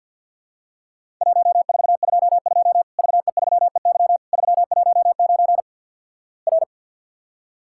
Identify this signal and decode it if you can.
Morse code, heard as a keyed tone, about 50 wpm, 700 Hz. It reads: 1422 VE3EY 319 R
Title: Worked All Europe DX Contest - QTC Exchange with Repeats - 50wpm